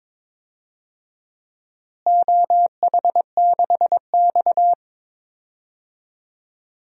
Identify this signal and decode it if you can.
Morse code, heard as a keyed tone, about 22 wpm, 700 Hz. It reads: OH6X
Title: DXCC-100 Call Signs - 4 Characters - Repeated 3x - 22wpm